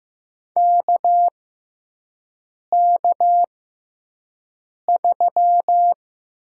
Morse code, heard as a keyed tone, about 15 wpm, spaced as for 5 wpm, 700 Hz. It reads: KK3
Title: USA Call Signs - 4 Characters - 6x Character Spacing - 15wpm